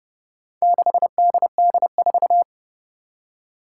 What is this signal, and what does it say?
Morse code, heard as a keyed tone, about 28 wpm, 690 Hz. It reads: 6DD4
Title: Four Challenging Letter-Number Combinations 30wpm